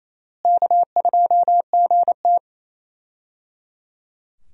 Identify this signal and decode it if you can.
Morse code, heard as a keyed tone, about 28 wpm, 700 Hz. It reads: K2GT
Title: USA Call Signs - 4 Characters - Repeated 3x - 28wpm